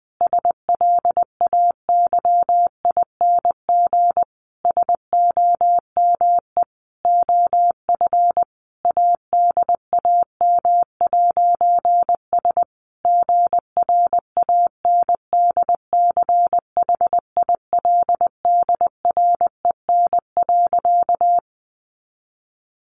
Morse code, heard as a keyed tone, about 20 wpm, 700 Hz. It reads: SLAYING SOME OF ADAM'S GRANDCHILDREN.
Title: A Sound Of Thunder 20wpm